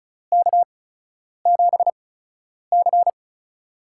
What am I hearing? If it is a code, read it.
Morse code, heard as a keyed tone, about 36 wpm, spaced as for 10 wpm, 700 Hz. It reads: K7C
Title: USA Call Signs - 4 Characters - 8x Character Spacing - 35wpm